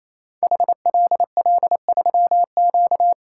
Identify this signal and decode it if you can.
Morse code, heard as a keyed tone, about 27 wpm, 700 Hz. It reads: HLL3Q
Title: Code Groups - 5 Letter:Numbers - 28wpm